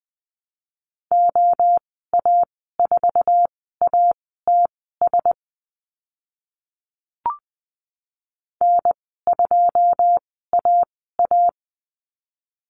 Morse code, heard as a keyed tone, about 20 wpm, spaced as for 15 wpm, 700 Hz. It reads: OA4ATS N2AA
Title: 082 Lesson 14g - Number 2 - Call Signs 20wpm